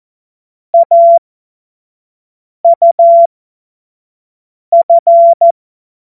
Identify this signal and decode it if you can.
Morse code, heard as a keyed tone, about 14 wpm, spaced as for 5 wpm, 680 Hz. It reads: AUF
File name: Morsekurs / Witze / Witz 27. Dezember 2022 25 F70